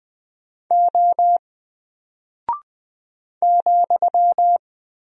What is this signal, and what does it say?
Morse code, heard as a keyed tone, about 20 wpm, 700 Hz. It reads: O ,
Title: Single Character (Letter:Number:Punctuation:Prosign) Alt 20wpm